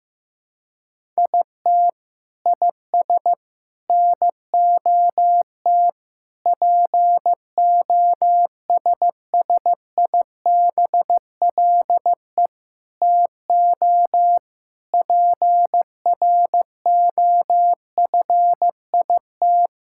Morse code, heard as a keyed tone, about 15 wpm, 700 Hz. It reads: IT IS NOT POSSIBLE TO PROFIT
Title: Sentences from Top 1000 Words - Review Part 5 - 15wpm